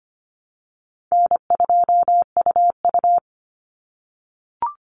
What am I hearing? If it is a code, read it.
Morse code, heard as a keyed tone, about 25 wpm, 700 Hz. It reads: N2UU
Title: Call Signs - 25wpm